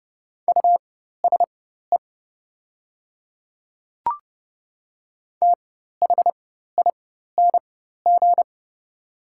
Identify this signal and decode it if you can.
Morse code, heard as a keyed tone, about 30 wpm, spaced as for 14 wpm, 700 Hz. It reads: USE THING
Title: 4x Character Spacing - Top 100 Words - 30wpm